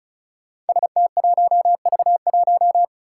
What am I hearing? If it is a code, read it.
Morse code, heard as a keyed tone, about 36 wpm, 700 Hz. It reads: ST1V1
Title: Five Challenging Letter-Number Combinations 35wpm